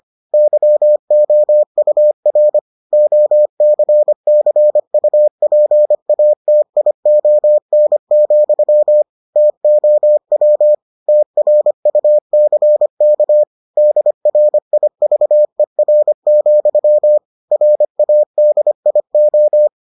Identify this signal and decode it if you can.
Morse code, heard as a keyed tone, about 25 wpm, 600 Hz. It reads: YOUR OCCUPATION, TOW TRUCK DRIVER, RADIO